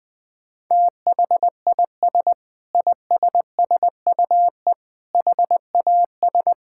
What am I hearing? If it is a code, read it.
Morse code, heard as a keyed tone, about 20 wpm, 700 Hz. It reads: THIS ISSUE HAS
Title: Sentences with 8-Word Phrases - Spoken Beginning - 20wpm